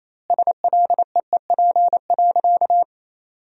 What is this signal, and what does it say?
Morse code, heard as a keyed tone, about 28 wpm, 700 Hz. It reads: SLEEP.
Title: 2x Word-Spacing - Sentences from Top 500 Words - 28wpm